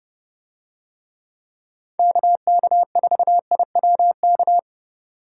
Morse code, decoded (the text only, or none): KK4IWK